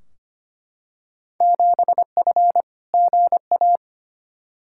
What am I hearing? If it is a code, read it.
Morse code, heard as a keyed tone, about 25 wpm, 700 Hz. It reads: 7F GA
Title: Field Day Exchanges Encore - 25wpm